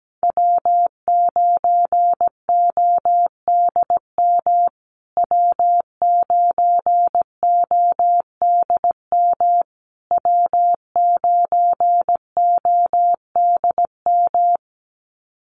Morse code, heard as a keyed tone, about 17 wpm, 700 Hz. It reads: W9ODM W9ODM W9ODM
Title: USA Call Signs Repeated 3x 17wpm